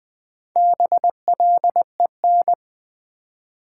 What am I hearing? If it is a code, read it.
Morse code, heard as a keyed tone, about 20 wpm, 700 Hz. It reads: BLEN
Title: Build A Word - 1 Letter At A Time - 5-Letter Words - 20wpm